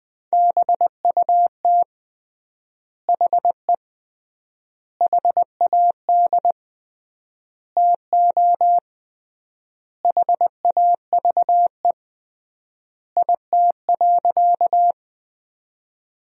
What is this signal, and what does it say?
Morse code, heard as a keyed tone, about 20 wpm, 700 Hz. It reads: BUT HE HAD TO HAVE IT.
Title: 3x Word-Spacing - Sentences from Top 100 Words - 20wpm